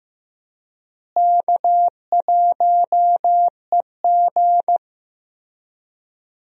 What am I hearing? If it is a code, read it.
Morse code, heard as a keyed tone, about 16 wpm, 700 Hz. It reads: K1EG
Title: USA Call Signs - 4 Characters - Repeated 3x - 15wpm